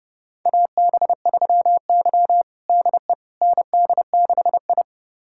Morse code, heard as a keyed tone, about 30 wpm, 700 Hz. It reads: AB3Y DE ND6I
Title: USA Callsign DE Callsign - 4 Characters - 30wpm